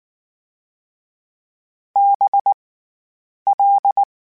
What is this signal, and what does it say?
Morse code, heard as a keyed tone, about 19 wpm, 800 Hz. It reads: B L
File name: Special INGs / Special INGs-mp3s / SlowToFaster ing @18